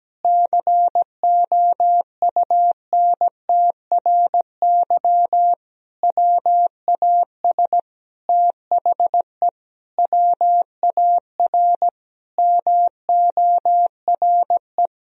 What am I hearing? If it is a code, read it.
Morse code, heard as a keyed tone, about 17 wpm, 700 Hz. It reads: COUNTRY WAS THE WAR MORE
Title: Sentences from Top 1000 Words - Review Part 3 - 17wpm